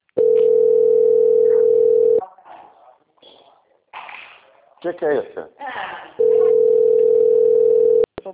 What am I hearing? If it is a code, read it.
Ringback tone (North America), 440 + 480 Hz, with speech.
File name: out- - - - - .18555